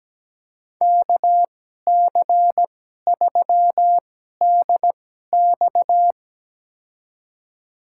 Morse code, heard as a keyed tone, about 17 wpm, spaced as for 12 wpm, 700 Hz. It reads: KC3DX
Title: USA Call Signs - 5 Characters - 2x Character Spacing - 17wpm